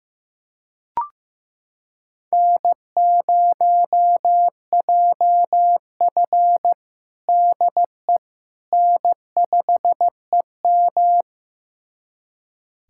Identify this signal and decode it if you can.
Morse code, heard as a keyed tone, about 15 wpm, 700 Hz. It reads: N0JF DE N5EM
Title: USA Callsign DE Callsign - 4 Characters - 15wpm